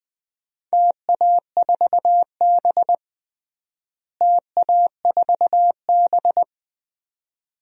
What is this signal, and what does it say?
Morse code, heard as a keyed tone, about 20 wpm, 700 Hz. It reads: TA4B TA4B
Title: DXCC-100 Call Signs - 4 Characters - Repeated 2x - 20wpm